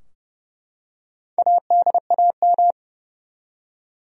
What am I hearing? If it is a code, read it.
Morse code, heard as a keyed tone, about 30 wpm, 700 Hz. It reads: ADAM
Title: USA Common Names - Max 5 Letters - 30wpm